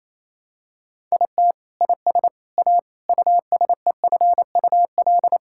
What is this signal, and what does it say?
Morse code, heard as a keyed tone, about 28 wpm, 700 Hz. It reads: IT IS A USEFUL
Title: Sentences from Top 1000 Words - Review Part 6 - 28wpm